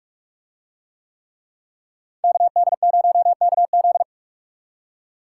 Morse code, heard as a keyed tone, about 46 wpm, 700 Hz. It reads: KD0KZ